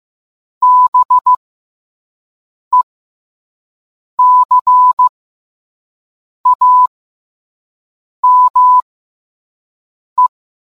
Morse code, heard as a keyed tone, about 15 wpm, spaced as for 5 wpm, 1000 Hz. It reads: BECAME